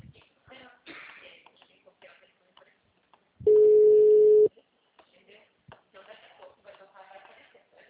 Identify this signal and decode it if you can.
Ringback tone (Europe, 425 Hz), with speech.